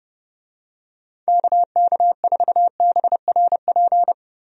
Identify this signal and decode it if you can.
Morse code, heard as a keyed tone, about 30 wpm, 700 Hz. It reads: KK4BRP
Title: USA Call Signs - 6 Characters - Repeated 2x - 30wpm